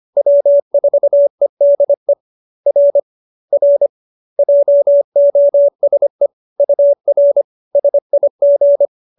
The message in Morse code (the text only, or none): W4EDE R R JOSE UR SIG